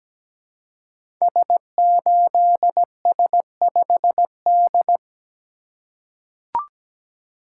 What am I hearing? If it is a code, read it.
Morse code, heard as a keyed tone, about 17 wpm, 700 Hz. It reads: S8S5D
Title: Code Groups - 5 Letter:Numbers - 17wpm